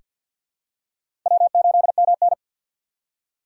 Morse code, heard as a keyed tone, about 47 wpm, 700 Hz. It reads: W8MN
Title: USA Call Signs - 4 Characters - 3x Character Spacing - 50wpm